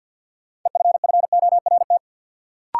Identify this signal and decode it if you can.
Morse code, heard as a keyed tone, about 53 wpm, 700 Hz. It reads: EFFORT
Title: Sets of 4 Words - New Word Drill - 50wpm